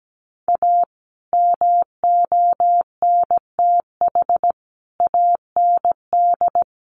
Morse code, heard as a keyed tone, about 17 wpm, 700 Hz. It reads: A MONTH AND
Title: Sentences from Top 800 Words 17wpm